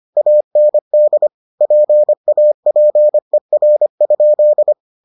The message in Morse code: AND PAPER?

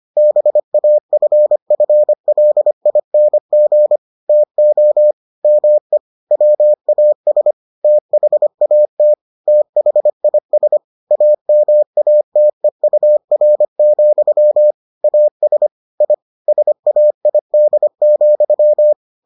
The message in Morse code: BAFFLING TO ME WAS THAT THIS AMATEUR, AS I SAID,

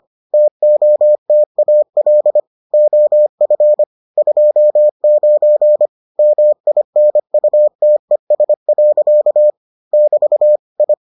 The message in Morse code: TOTAL OF 29 MINUTES. <BT> I

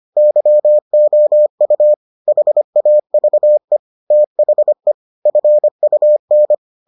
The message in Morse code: YOU HAVE THE FUN

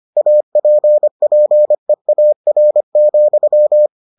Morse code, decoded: APPEAR,

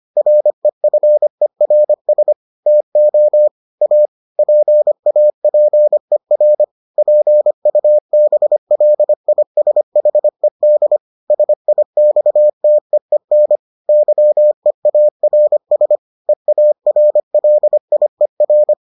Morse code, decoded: REFERS TO A PAPER PUBLISHED SIXTEEN YEARS EARLIER

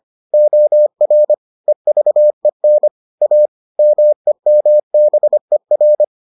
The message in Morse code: OR EVEN A MEMBER

